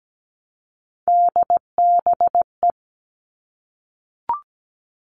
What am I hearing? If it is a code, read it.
Morse code, heard as a keyed tone, about 17 wpm, 700 Hz. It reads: DBE
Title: All Cut Numbers - 3-Digit 17wpm